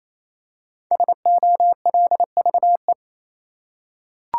Morse code, heard as a keyed tone, about 28 wpm, 700 Hz. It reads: SOLVE
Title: Five Letter Words - 28wpm